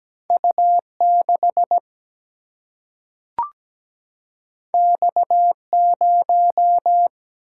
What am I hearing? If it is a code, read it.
Morse code, heard as a keyed tone, about 17 wpm, 700 Hz. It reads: U6 X0
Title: Single-Letter Single-Number 17wpm